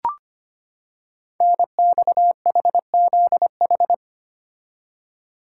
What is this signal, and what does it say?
Morse code, heard as a keyed tone, about 25 wpm, 690 Hz. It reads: NXHZH
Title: Code Groups - 5 Letters - 25wpm